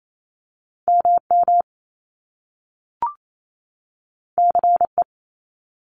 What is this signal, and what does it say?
Morse code, heard as a keyed tone, about 28 wpm, 700 Hz. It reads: MM CE
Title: Top Two Letter Combinations in Words 28wpm